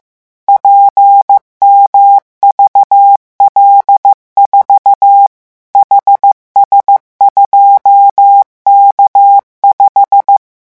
Morse code, heard as a keyed tone, about 15 wpm, 800 Hz. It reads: PMVL4 HS2K5
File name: morse mp3 / revision12-vitesse 14 mots